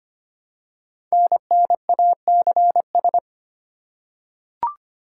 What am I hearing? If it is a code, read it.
Morse code, heard as a keyed tone, about 25 wpm, 700 Hz. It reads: NNACS
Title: Code Groups - 5 Letters - 25wpm